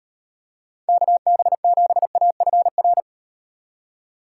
Morse code, heard as a keyed tone, about 38 wpm, 700 Hz. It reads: KB7AFR